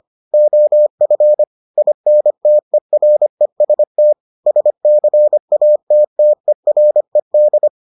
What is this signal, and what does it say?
Morse code, heard as a keyed tone, about 25 wpm, 600 Hz. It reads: OF INTEREST SCATTERED